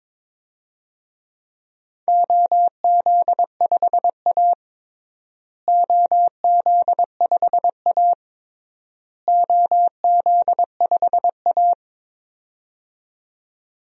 Morse code, heard as a keyed tone, about 22 wpm, 700 Hz. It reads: OZ5A OZ5A OZ5A
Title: DXCC-100 Call Signs - 4 Characters - Repeated 3x - 22wpm